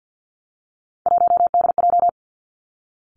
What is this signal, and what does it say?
Morse code, heard as a keyed tone, about 50 wpm, 700 Hz. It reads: 1DJ
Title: Three Challenging Letter-Number Combinations 50wpm